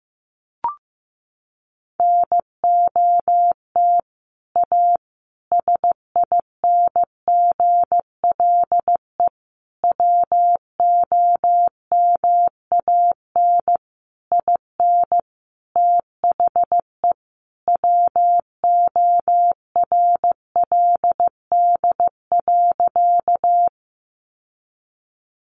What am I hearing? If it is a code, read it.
Morse code, heard as a keyed tone, about 15 wpm, 700 Hz. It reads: NOT A SINGLE WOMAN IN THE WORLD.
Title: Sentences from Top 1100 Words 15wpm